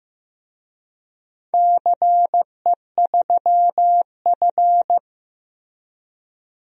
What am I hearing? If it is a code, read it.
Morse code, heard as a keyed tone, about 15 wpm, 700 Hz. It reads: CE3F
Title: DXCC-100 Call Signs - 4 Characters - 15wpm